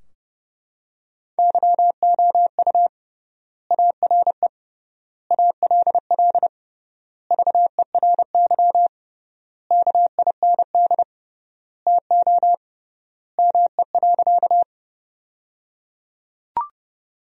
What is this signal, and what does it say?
Morse code, heard as a keyed tone, about 30 wpm, 700 Hz. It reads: YOU ARE ALL VERY KIND TO ME.
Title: 3x Word-Spacing - Sentences from Top 200 Words - 30wpm